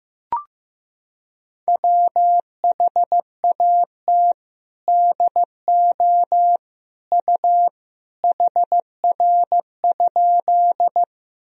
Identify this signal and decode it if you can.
Morse code, heard as a keyed tone, about 15 wpm, 700 Hz. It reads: WHAT DO U HR?
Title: Sentences from Top 300 Words and Abbreviations 15wpm